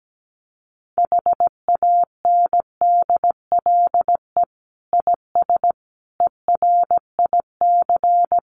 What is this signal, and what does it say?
Morse code, heard as a keyed tone, about 17 wpm, 700 Hz. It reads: HANDLE IS ERIC
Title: Common QSO Phrases 17wpm